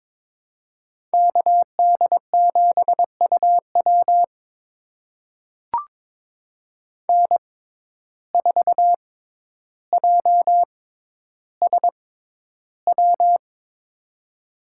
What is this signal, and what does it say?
Morse code, heard as a keyed tone, about 22 wpm, spaced as for 11 wpm, 700 Hz. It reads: KD7UW N4JSW
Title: USA Call Signs - 5 Characters - 6x Character Spacing - 22wpm